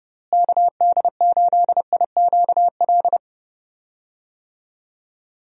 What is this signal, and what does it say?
Morse code, heard as a keyed tone, about 30 wpm, 700 Hz. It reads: KD8IQL
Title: USA Call Signs - 6 Characters - Repeated 3x - 30wpm